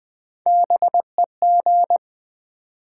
Morse code, heard as a keyed tone, about 20 wpm, 700 Hz. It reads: BEG
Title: Build A Word - 3 Letters At A Time - 9-Letter Words - 20wpm